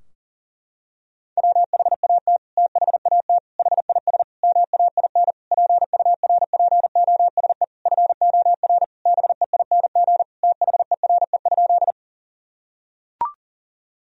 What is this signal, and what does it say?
Morse code, heard as a keyed tone, about 37 wpm, 700 Hz. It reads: WHAT THAT HIS MAIN PURPOSE FOR BEING THERE?
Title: Sentences from Top 1000 Words - Review Part 2 - 40wpm